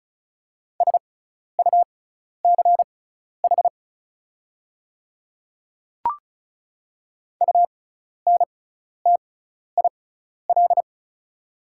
Morse code, heard as a keyed tone, about 36 wpm, spaced as for 12 wpm, 690 Hz. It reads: SUCH UNTIL
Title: 6x Character Spacing - Top 300 Words - 35wpm